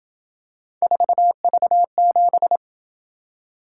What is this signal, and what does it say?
Morse code, heard as a keyed tone, about 27 wpm, 700 Hz. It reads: 4V7